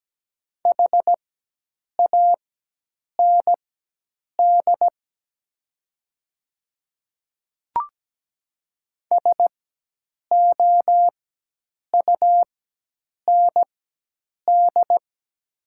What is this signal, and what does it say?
Morse code, heard as a keyed tone, about 17 wpm, spaced as for 8 wpm, 700 Hz. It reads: HAND SOUND
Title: 4x Character Spacing - Top 200 Words - 17wpm